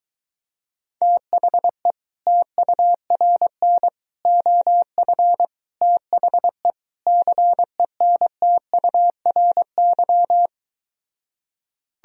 Morse code, heard as a keyed tone, about 22 wpm, 700 Hz. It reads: THE TURN OF THE CENTURY